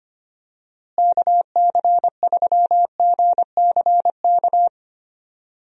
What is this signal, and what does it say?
Morse code, heard as a keyed tone, about 25 wpm, 700 Hz. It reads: KC3GCK
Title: USA Call Signs - 6 Characters - Repeated 3x - 25wpm